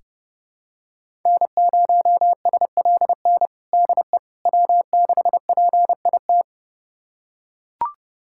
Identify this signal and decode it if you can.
Morse code, heard as a keyed tone, about 32 wpm, 700 Hz. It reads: N0SLN DE W6PIT
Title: USA Callsign DE Callsign - 5 Characters - 30wpm